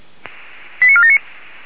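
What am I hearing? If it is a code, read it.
Five-tone selective call (ZVEI), heard as quick runs of five tones, tones of about 70 ms. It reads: 89479
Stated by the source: ZVEI five-tone selective call received by SDR.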